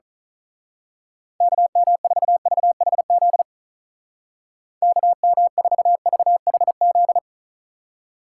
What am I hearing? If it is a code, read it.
Morse code, heard as a keyed tone, about 38 wpm, 700 Hz. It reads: KM4VHZ KM4VHZ
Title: Speed Racing - USA 2x3 Call Signs 35wpm